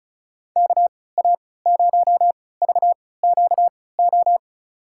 Morse code, heard as a keyed tone, about 36 wpm, spaced as for 20 wpm, 700 Hz. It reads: KA0VQO
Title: USA Call Signs - 6 Characters - 3x Character Spacing - 35wpm